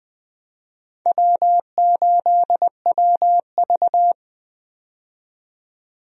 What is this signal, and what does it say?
Morse code, heard as a keyed tone, about 20 wpm, 700 Hz. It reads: W8WV
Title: USA Call Signs - 4 Characters - 20wpm